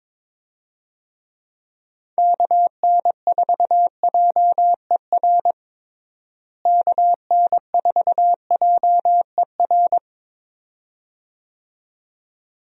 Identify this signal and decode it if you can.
Morse code, heard as a keyed tone, about 22 wpm, 700 Hz. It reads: KN4JER KN4JER
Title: USA Call Signs - 6 Characters - Repeated 2x - 22wpm